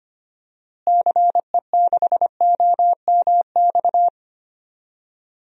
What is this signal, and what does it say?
Morse code, heard as a keyed tone, about 25 wpm, 700 Hz. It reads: CE6OMX
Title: DXCC-100 Call Signs - 6 Characters - Repeated 2x - 25wpm